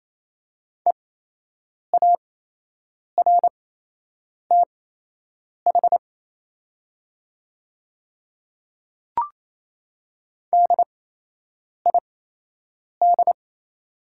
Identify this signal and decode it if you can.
Morse code, heard as a keyed tone, about 28 wpm, spaced as for 8 wpm, 700 Hz. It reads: EARTH DID
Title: 8x Character Spacing - Top 300 Words - 28wpm